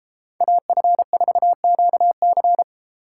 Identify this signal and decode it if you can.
Morse code, heard as a keyed tone, about 33 wpm, 700 Hz. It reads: AF4QC